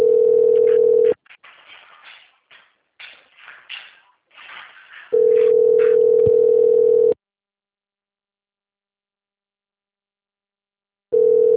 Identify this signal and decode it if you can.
Ringback tone (North America), 440 + 480 Hz.